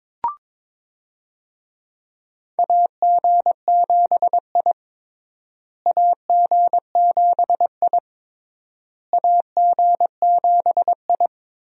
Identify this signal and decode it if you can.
Morse code, heard as a keyed tone, about 22 wpm, 700 Hz. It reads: AG7I AG7I AG7I
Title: USA Call Signs - 4 Characters - Repeated 3x - 22wpm